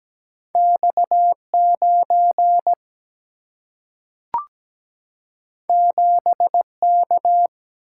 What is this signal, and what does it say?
Morse code, heard as a keyed tone, about 17 wpm, 700 Hz. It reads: X9 7K
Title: Single-Letter Single-Number 17wpm